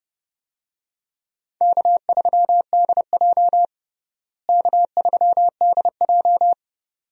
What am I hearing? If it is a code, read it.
Morse code, heard as a keyed tone, about 29 wpm, 700 Hz. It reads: K3DJ K3DJ
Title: USA Call Signs - 4 Characters - Repeated 2x - 30wpm